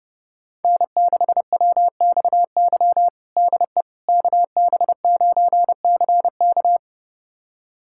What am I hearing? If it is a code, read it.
Morse code, heard as a keyed tone, about 30 wpm, 700 Hz. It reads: N6WXY DE KB9CK
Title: USA Callsign DE Callsign - 5 Characters - 30wpm